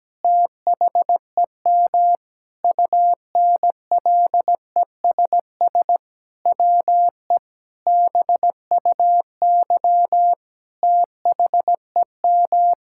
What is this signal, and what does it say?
Morse code, heard as a keyed tone, about 17 wpm, 700 Hz. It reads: THEM UNLESS WE BUY THEM
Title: Sentences from Top 1000 Words - Review Part 6 - 17wpm